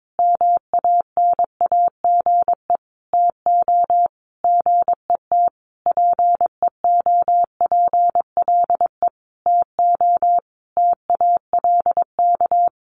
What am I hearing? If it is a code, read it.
Morse code, heard as a keyed tone, about 22 wpm, 700 Hz. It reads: MANAGE TO GET PEOPLE TO TALK
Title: Sentences from Top 1400 Words 22wpm